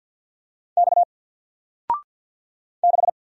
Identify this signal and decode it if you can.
Morse code, heard as a keyed tone, about 50 wpm, 700 Hz. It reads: X 6
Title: Single Letter-Number Alt 50wpm